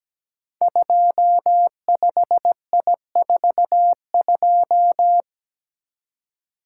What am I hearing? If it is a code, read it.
Morse code, heard as a keyed tone, about 17 wpm, 700 Hz. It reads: 25I42
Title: Code Groups - 5 Letter:Numbers - 17wpm